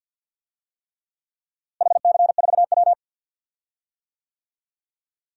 Morse code, heard as a keyed tone, about 47 wpm, 700 Hz. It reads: HC4W
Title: DXCC-100 Call Signs - 4 Characters - Repeated 3x - 50wpm